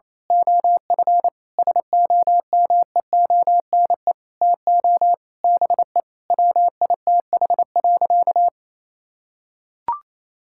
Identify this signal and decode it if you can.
Morse code, heard as a keyed tone, about 28 wpm, 700 Hz. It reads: OF SOMEONE TO BE WITH.
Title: Sentences from Top 1000 Words - Review Part 5 - 28wpm